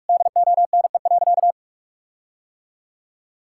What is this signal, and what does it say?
Morse code, heard as a keyed tone, about 46 wpm, 700 Hz. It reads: DONE.
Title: Sentences from Top 1600 Words 45wpm